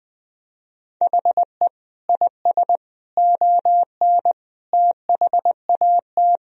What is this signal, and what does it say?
Morse code, heard as a keyed tone, about 20 wpm, 700 Hz. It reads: HE IS ON THAT
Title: Sentences from Top 1000 Words - Review Part 1 - 20wpm